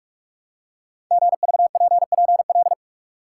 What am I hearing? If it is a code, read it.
Morse code, heard as a keyed tone, about 44 wpm, 690 Hz. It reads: GVPPL
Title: Code Groups - 5 Letters - 45wpm